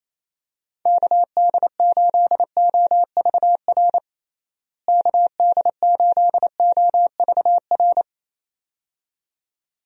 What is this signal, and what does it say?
Morse code, heard as a keyed tone, about 28 wpm, 700 Hz. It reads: KD8OVR KD8OVR
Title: USA Call Signs - 6 Characters - Repeated 2x - 28wpm